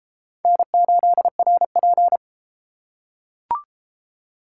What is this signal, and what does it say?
Morse code, heard as a keyed tone, about 29 wpm, 700 Hz. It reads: N8RP